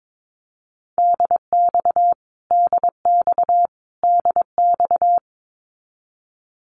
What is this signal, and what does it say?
Morse code, heard as a keyed tone, about 22 wpm, 700 Hz. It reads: DX DX DX
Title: Abbreviations Repeated 3x - 22wpm